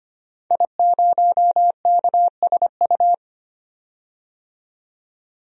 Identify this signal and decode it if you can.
Morse code, heard as a keyed tone, about 25 wpm, 700 Hz. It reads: I0KSU
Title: DXCC-100 Call Signs - 5 Characters - 25wpm